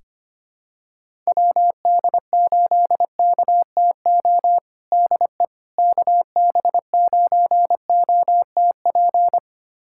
Morse code, heard as a keyed tone, about 25 wpm, 700 Hz. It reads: WD8KTO DE KB9OTP
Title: USA Callsign DE Callsign - 6 Characters - 25wpm